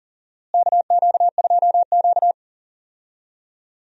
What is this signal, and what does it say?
Morse code, heard as a keyed tone, about 37 wpm, 700 Hz. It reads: KQ2Q
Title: USA Call Signs - 4 Characters - Repeated 2x - 40wpm